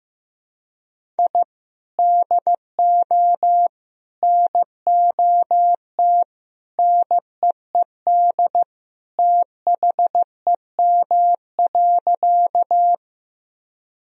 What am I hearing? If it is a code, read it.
Morse code, heard as a keyed tone, about 16 wpm, 700 Hz. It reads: I DO NOT NEED THEM.
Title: Sentences from Top 1000 Words - Review Part 1 - 15wpm